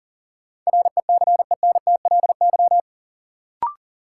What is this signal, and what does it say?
Morse code, heard as a keyed tone, about 40 wpm, 700 Hz. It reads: RECENTLY
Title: Sentences from Top 1200 Words - New Word Drill Alt - 40wpm